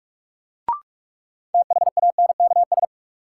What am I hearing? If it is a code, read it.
Morse code, heard as a keyed tone, about 46 wpm, 700 Hz. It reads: THANKS
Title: Sets of 5 Words - New Word Drill - 45wpm